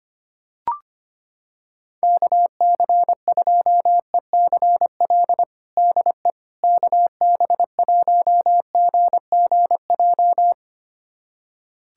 Morse code, heard as a keyed tone, about 25 wpm, 700 Hz. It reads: KC2ECL DE KB1GGJ
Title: USA Callsign DE Callsign - 6 Characters - 25wpm